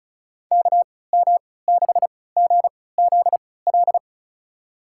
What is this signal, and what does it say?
Morse code, heard as a keyed tone, about 36 wpm, spaced as for 20 wpm, 700 Hz. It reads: KM6GZL
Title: USA Call Signs - 6 Characters - 3x Character Spacing - 35wpm